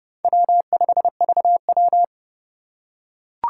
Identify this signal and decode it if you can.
Morse code, heard as a keyed tone, about 29 wpm, 700 Hz. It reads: W5VW